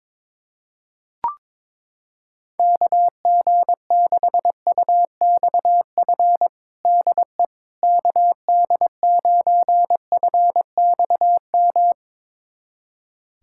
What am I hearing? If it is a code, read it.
Morse code, heard as a keyed tone, about 22 wpm, 700 Hz. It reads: KG6UXF DE KD9FXM
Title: USA Callsign DE Callsign 22wpm